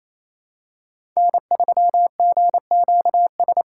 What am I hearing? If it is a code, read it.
Morse code, heard as a keyed tone, about 28 wpm, 700 Hz. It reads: N3GQS
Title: USA Call Signs - 8x Character Spacing - 28wpm